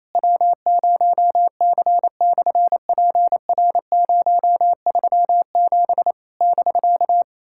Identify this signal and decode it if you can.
Morse code, heard as a keyed tone, about 28 wpm, 700 Hz. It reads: W0C/PR037 <BK>
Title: SOTA QSO Practice 28wpm